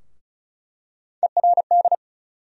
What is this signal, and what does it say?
Morse code, heard as a keyed tone, about 36 wpm, 700 Hz. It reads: ERD
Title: USA Call Sign Postfixes - 3 Letters 35wpm